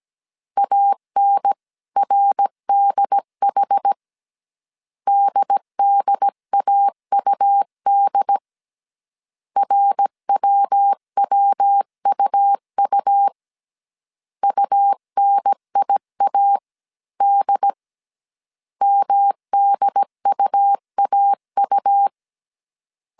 Morse code, heard as a keyed tone, about 17 wpm, 800 Hz. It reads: ANRDH DDAUD RWWUU UNIAD MDUAU